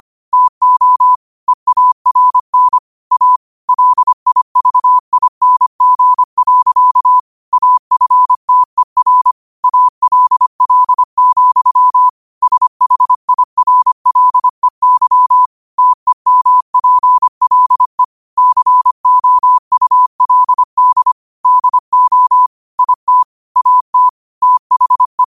Morse code, heard as a keyed tone, about 25 wpm, 1000 Hz. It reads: TO EARN A LIVING. AFTER ALL, SHIRLEY TEMPLE COULD DO IT AT THE
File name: qotdfiles / 25wpm